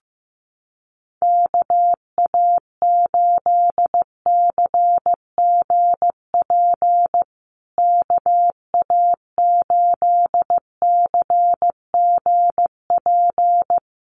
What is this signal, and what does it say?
Morse code, heard as a keyed tone, about 15 wpm, 700 Hz. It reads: KA8CGP KA8CGP
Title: USA Call Signs Repeated 2x 15wpm